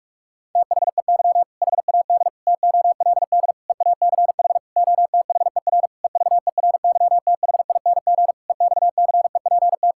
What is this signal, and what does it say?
Morse code, heard as a keyed tone, about 46 wpm, 700 Hz. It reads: THEY HAD TOLD EACH OTHER EVERYTHING EXCEPT
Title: Sentences from Top 1600 Words 45wpm